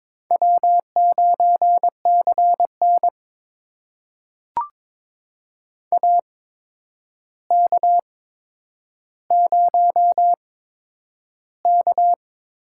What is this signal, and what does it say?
Morse code, heard as a keyed tone, about 22 wpm, spaced as for 9 wpm, 700 Hz. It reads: W9CN AK0K
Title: USA Call Signs - 4 Characters - 8x Character Spacing - 22wpm